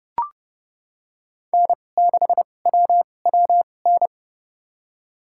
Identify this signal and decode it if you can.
Morse code, heard as a keyed tone, about 30 wpm, spaced as for 22 wpm, 700 Hz. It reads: N6WWN
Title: USA Call Signs - 5 Characters - 2x Character Spacing - 30wpm